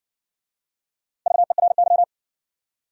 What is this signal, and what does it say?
Morse code, heard as a keyed tone, about 58 wpm, 700 Hz. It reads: VERY